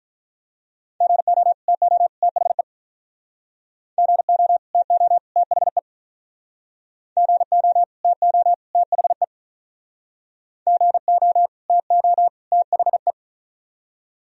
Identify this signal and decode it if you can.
Morse code, heard as a keyed tone, about 47 wpm, 700 Hz. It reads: GO TO THE GO TO THE GO TO THE GO TO THE